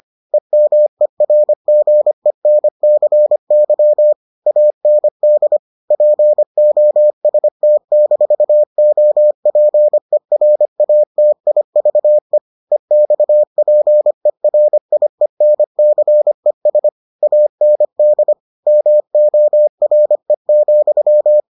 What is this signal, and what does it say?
Morse code, heard as a keyed tone, about 25 wpm, 600 Hz. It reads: EMERGENCY AND POST-OPERATIVE EXPERIENCES AND MORE,